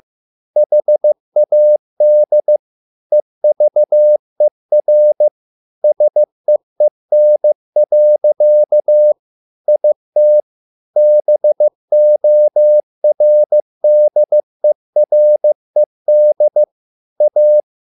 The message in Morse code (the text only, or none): HAD EVER SEEN. IT BORDERED A